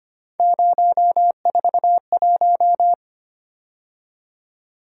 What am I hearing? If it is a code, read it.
Morse code, heard as a keyed tone, about 25 wpm, 700 Hz. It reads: 041
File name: Three Letter-Number Combinations - 25wpm